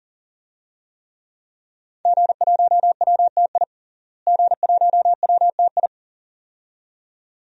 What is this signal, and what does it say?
Morse code, heard as a keyed tone, about 40 wpm, 700 Hz. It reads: G1WTI G1WTI
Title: DXCC-100 Call Signs - 5 Characters - Repeated 2x - 40wpm